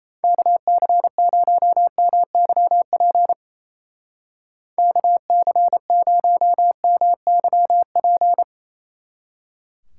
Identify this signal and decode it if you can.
Morse code, heard as a keyed tone, about 36 wpm, 700 Hz. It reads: KC0MYP KC0MYP